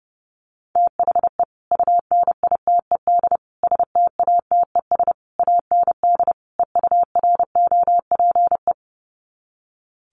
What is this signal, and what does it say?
Morse code, heard as a keyed tone, about 30 wpm, 700 Hz. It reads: THE UNITED STATES AND EUROPE
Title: Sets of 5 Words Encore 30wpm